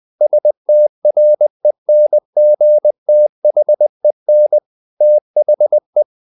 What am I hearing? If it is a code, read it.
Morse code, heard as a keyed tone, about 20 wpm, 600 Hz. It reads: STRENGTHEN THE
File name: the old man and the sea / the old man and the sea part 1 20wpm